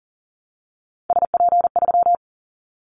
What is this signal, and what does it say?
Morse code, heard as a keyed tone, about 40 wpm, 700 Hz. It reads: SP3
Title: Callsign Prefixes for Top 25 DXCC - Repeated 3x - 40wpm